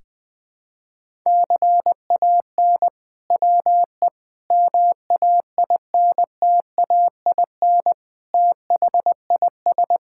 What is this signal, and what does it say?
Morse code, heard as a keyed tone, about 20 wpm, 700 Hz. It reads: CAN WE MAINTAIN THIS
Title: Sentences from Top 1000 Words - Review Part 5 - 20wpm